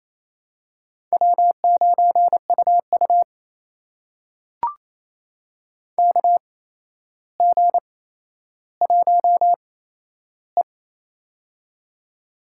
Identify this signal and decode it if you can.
Morse code, heard as a keyed tone, about 28 wpm, spaced as for 12 wpm, 700 Hz. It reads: W9UU KG1E
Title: USA Call Signs - 4 Characters - 8x Character Spacing - 28wpm